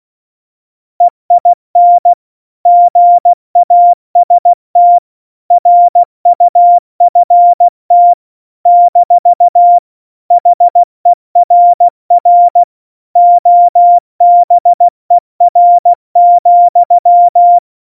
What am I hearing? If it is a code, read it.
Morse code, heard as a keyed tone, about 16 wpm, 700 Hz. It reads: EIN GAST RUFT - HERR OBER,
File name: Morsekurs / Witze / Witz 03. September 2019, 70 F80